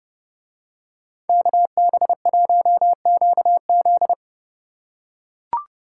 Morse code, heard as a keyed tone, about 31 wpm, 700 Hz. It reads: KB1QZ